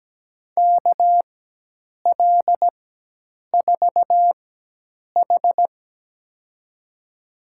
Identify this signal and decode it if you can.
Morse code, heard as a keyed tone, about 17 wpm, spaced as for 8 wpm, 700 Hz. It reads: KL4H
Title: USA Call Signs - 4 Characters - 4x Character Spacing - 17wpm